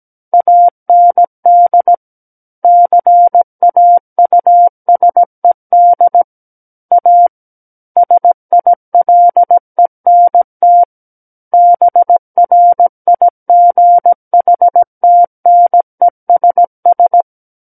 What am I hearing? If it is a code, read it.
Morse code, heard as a keyed tone, about 17 wpm, 700 Hz. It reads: AND CAUSED A SILENT BRIGHTNESS
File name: Book One / 08-Chapter 8 - 17 WPM